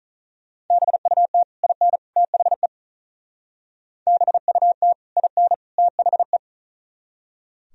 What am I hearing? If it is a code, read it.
Morse code, heard as a keyed tone, about 43 wpm, 700 Hz. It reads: BUT IN THE BUT IN THE
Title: Speed-Racing - Sets of 3 Words 35wpm